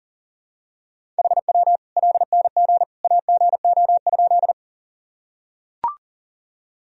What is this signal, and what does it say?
Morse code, heard as a keyed tone, about 40 wpm, 700 Hz. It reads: HW LNG AGO?
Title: Sentences from Top 500 Words and Abbreviations 40wpm